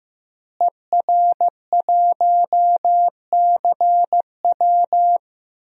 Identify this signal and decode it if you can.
Morse code, heard as a keyed tone, about 15 wpm, 700 Hz. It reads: ER1CW
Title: DXCC-100 Call Signs - 5 Characters - Repeated 3x - 15wpm